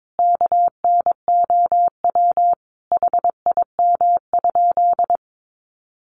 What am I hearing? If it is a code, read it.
Morse code, heard as a keyed tone, about 22 wpm, 700 Hz. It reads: KNOW HIM?
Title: Sentences from Top 100 Words 22wpm